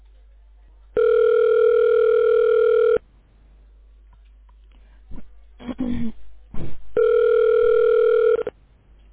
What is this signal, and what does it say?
Ringback tone (North America), 440 + 480 Hz.